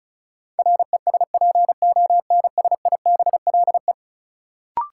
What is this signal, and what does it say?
Morse code, heard as a keyed tone, about 34 wpm, 700 Hz. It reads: RESPONSIBLE
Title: Sentences from Top 1000 Words - Review Part 7 - New Word Drill - 35wpm